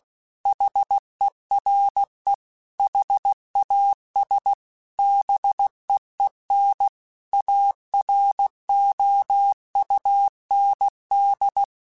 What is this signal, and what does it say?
Morse code, heard as a keyed tone, about 16 wpm, 800 Hz. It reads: HERE HAS BEEN AROUND